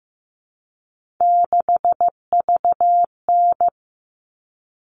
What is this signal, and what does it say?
Morse code, heard as a keyed tone, about 15 wpm, 700 Hz. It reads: 6VN
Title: All Cut Numbers - 3-Digit 15wpm